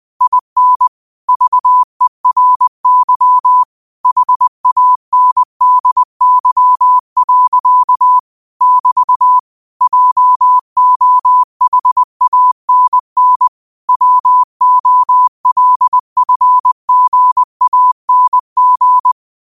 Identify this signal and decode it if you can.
Morse code, heard as a keyed tone, about 20 wpm, 1000 Hz. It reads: IN VERY HANDY. <BT> JOHANN WOLFGANG
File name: qotdfiles / 20wpm